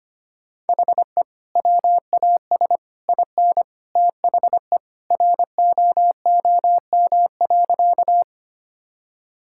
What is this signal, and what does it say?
Morse code, heard as a keyed tone, about 25 wpm, 700 Hz. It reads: HE WAS IN THE ROOM.
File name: practice / 25wpm / Sentences with 2-Word Phrases - 25wpm